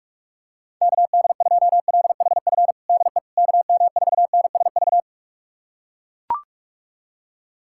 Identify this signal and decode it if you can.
Morse code, heard as a keyed tone, about 42 wpm, 700 Hz. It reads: KD2LHF DE KM4NSV
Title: USA Callsign DE Callsign - 6 Characters - 45wpm